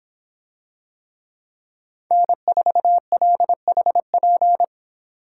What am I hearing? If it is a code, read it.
Morse code, heard as a keyed tone, about 24 wpm, 700 Hz. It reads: N4LHP